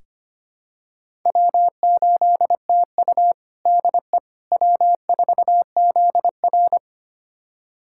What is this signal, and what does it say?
Morse code, heard as a keyed tone, about 25 wpm, 700 Hz. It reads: W8TU DE W4ZR
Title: USA Callsign DE Callsign - 4 Characters - 25wpm